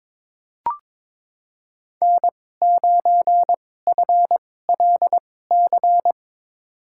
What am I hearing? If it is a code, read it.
Morse code, heard as a keyed tone, about 22 wpm, spaced as for 16 wpm, 700 Hz. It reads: N9FLC
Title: USA Call Signs - 5 Characters - 2x Character Spacing - 22wpm